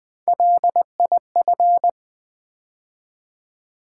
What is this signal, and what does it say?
Morse code, heard as a keyed tone, about 20 wpm, 700 Hz. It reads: LIF
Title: Build A Word - 3 Letters At A Time - 8-Letter Words - 20wpm